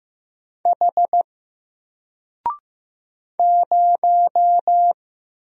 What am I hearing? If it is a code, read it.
Morse code, heard as a keyed tone, about 15 wpm, 700 Hz. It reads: H 0
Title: Single Character (Letter:Number:Punctuation:Prosign) 15wpm